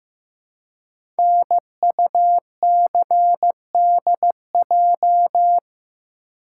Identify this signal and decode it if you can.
Morse code, heard as a keyed tone, about 15 wpm, 700 Hz. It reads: NUCDJ
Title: Code Groups - 5 Letters - 15wpm